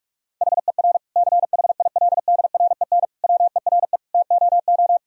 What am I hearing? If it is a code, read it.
Morse code, heard as a keyed tone, about 46 wpm, 700 Hz. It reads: HER CHILDREN WERE TOO
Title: Sentences from Top 1000 Words - Review Part 4 - 45wpm